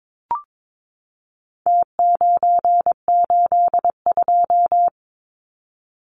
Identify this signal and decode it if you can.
Morse code, heard as a keyed tone, about 22 wpm, 700 Hz. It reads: T982
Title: Numbers - Contest Style - 4-Digit 22wpm